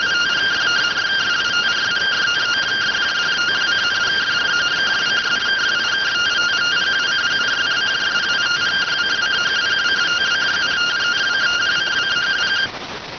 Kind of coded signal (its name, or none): two-tone teleprinter signal